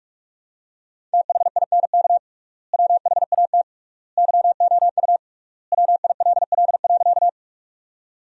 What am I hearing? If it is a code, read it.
Morse code, heard as a keyed tone, about 43 wpm, 700 Hz. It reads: THINK WHAT YOU WILL.
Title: 3x Word-Spacing - Sentences from Top 200 Words - 45wpm